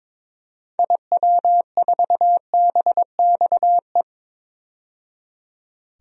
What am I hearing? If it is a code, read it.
Morse code, heard as a keyed tone, about 22 wpm, 700 Hz. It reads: IW4BXE
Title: DXCC-100 Call Signs - 6 Characters - 22wpm